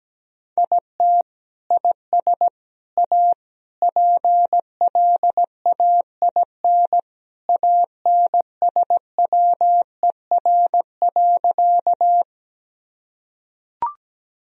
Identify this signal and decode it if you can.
Morse code, heard as a keyed tone, about 17 wpm, 700 Hz. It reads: IT IS A PLAIN ANSWER.
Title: Sentences from Top 1000 Words - Review Part 2 - 17wpm